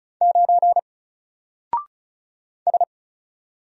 Morse code, heard as a keyed tone, about 36 wpm, 700 Hz. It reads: 9 S
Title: Single Letter-Number Alt 35wpm